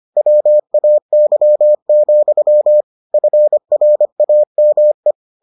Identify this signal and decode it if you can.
Morse code, heard as a keyed tone, about 25 wpm, 600 Hz. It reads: WAY, FRAME